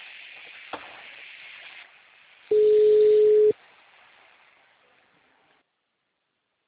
Ringback tone (Europe, 425 Hz).
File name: out- -2005- - - .86446